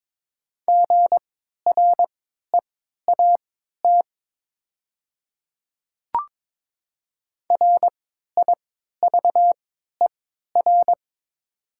Morse code, heard as a keyed tone, about 22 wpm, spaced as for 12 wpm, 700 Hz. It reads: GREAT RIVER
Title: 3x Character Spacing - Top 300 Words - 22wpm